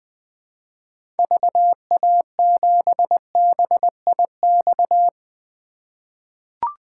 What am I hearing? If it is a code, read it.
Morse code, heard as a keyed tone, about 21 wpm, 700 Hz. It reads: VA7BIX
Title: 2023 POTA Activator Call Signs - 20wpm